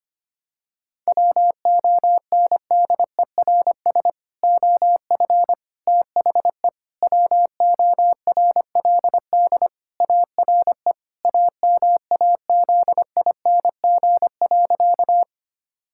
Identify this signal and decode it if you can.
Morse code, heard as a keyed tone, about 25 wpm, 700 Hz. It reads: WONDERS OF THE WORLD ARE AMAZING.
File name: practice / 25wpm / Sentences with 3-Word Phrases - 25wpm